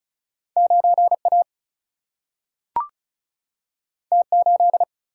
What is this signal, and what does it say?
Morse code, heard as a keyed tone, about 36 wpm, 700 Hz. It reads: 9 A T 8
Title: Single-Letter Single-Number 35wpm